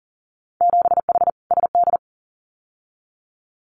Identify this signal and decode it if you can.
Morse code, heard as a keyed tone, about 40 wpm, 700 Hz. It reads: 7H SD
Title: Winter Field Day Exchanges Encore 40wpm